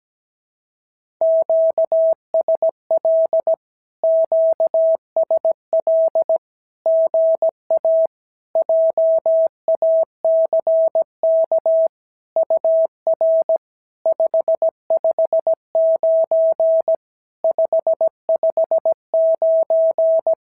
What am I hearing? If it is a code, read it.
Morse code, heard as a keyed tone, about 17 wpm, 650 Hz. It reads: QSL QSL GA JACK UR 559 559
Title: SOTA QSO Practice 17wpm